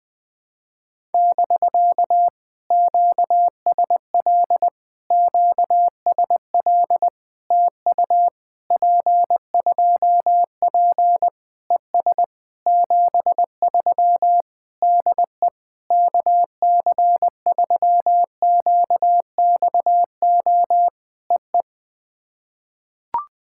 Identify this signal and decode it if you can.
Morse code, heard as a keyed tone, about 20 wpm, 700 Hz. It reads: <BK> QSL QSL TU P2P ES 73 DE KC3QXO EE
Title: POTA QSO Practice - 20wpm